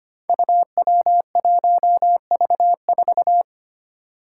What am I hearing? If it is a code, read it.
Morse code, heard as a keyed tone, about 25 wpm, 700 Hz. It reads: UW1V4
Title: Five Challenging Letter-Number Combinations 25wpm